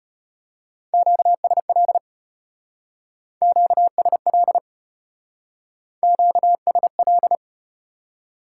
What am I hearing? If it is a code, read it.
Morse code, heard as a keyed tone, about 34 wpm, 700 Hz. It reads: QSL QSL QSL